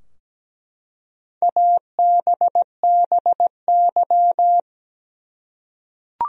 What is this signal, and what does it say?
Morse code, heard as a keyed tone, about 17 wpm, 700 Hz. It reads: ABBY
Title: USA Common Names - 17wpm